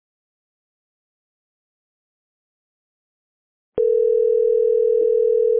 Ringback tone (North America), 440 + 480 Hz.